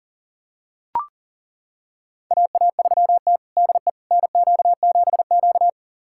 Morse code, heard as a keyed tone, about 40 wpm, 700 Hz. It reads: AA3T DE NQ7Q
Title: USA Callsign DE Callsign - 4 Characters - 40wpm